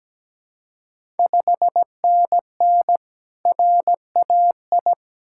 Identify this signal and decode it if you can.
Morse code, heard as a keyed tone, about 17 wpm, 690 Hz. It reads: 5NN RAI
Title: Ontario QSO Party Exchange 17wpm